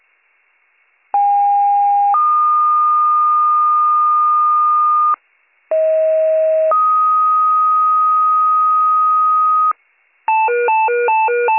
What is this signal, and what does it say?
Two-tone page, heard as a one-second tone, then a three-second tone, about 810 then 1220 Hz, about 640 then 1250 Hz. It is heard from 1.1 to 9.7 s.